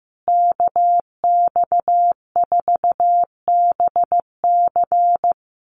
Morse code, heard as a keyed tone, about 15 wpm, 700 Hz. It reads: KX4BC
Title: USA Call Signs Repeated 2x 15wpm